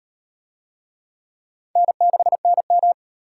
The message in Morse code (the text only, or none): N6NM